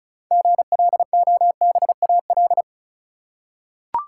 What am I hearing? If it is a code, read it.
Morse code, heard as a keyed tone, about 36 wpm, 700 Hz. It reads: GLOBAL
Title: Six Letter Words - 35wpm